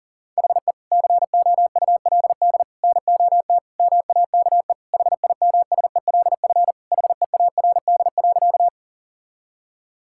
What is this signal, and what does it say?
Morse code, heard as a keyed tone, about 40 wpm, 700 Hz. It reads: HE COULD NOT MAKE HIMSELF HEARD.
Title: Sentences from Top 1600 Words 40wpm